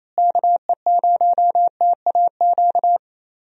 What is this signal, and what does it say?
Morse code, heard as a keyed tone, about 29 wpm, 700 Hz. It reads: KE0TAQ
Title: USA Call Signs - 6 Characters - Repeated 2x - 28wpm